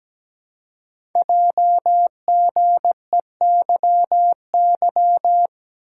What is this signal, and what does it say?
Morse code, heard as a keyed tone, about 17 wpm, 700 Hz. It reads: JGEYY
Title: Code Groups - 5 Letters - 17wpm